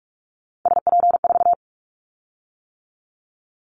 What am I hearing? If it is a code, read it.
Morse code, heard as a keyed tone, about 44 wpm, 700 Hz. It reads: SP4
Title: Callsign Prefixes for Top 25 DXCC - 45wpm